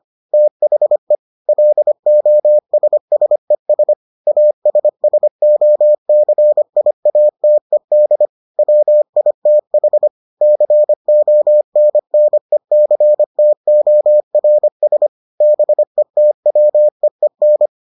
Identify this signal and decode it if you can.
Morse code, heard as a keyed tone, about 25 wpm, 600 Hz. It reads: THE LOSSES ASSOCIATED WITH CONNECTORS BETWEEN